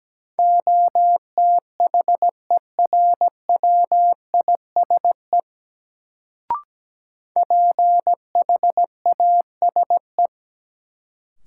Morse code, heard as a keyed tone, about 17 wpm, 700 Hz. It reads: OTHERWISE PHASE
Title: Sentences from Top 1200 Words - New Word Drill Alt - 17wpm